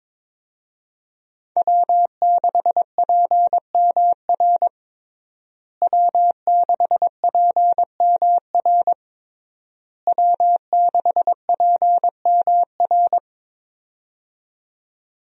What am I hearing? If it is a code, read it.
Morse code, heard as a keyed tone, about 22 wpm, 700 Hz. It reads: W6PMR W6PMR W6PMR
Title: USA Call Signs - 5 Characters - Repeated 3x - 22wpm